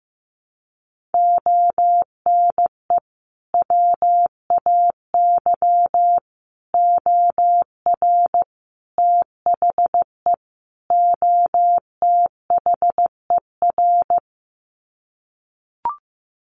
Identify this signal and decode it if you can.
Morse code, heard as a keyed tone, about 15 wpm, 700 Hz. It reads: ONE WAY OR THE OTHER
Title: Sets of 5 Words 15wpm